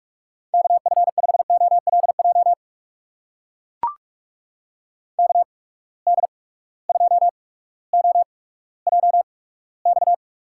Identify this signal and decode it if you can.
Morse code, heard as a keyed tone, about 42 wpm, spaced as for 19 wpm, 700 Hz. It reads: KF5OLJ KD2OJX
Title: USA Call Signs - 6 Characters - 8x Character Spacing - 45wpm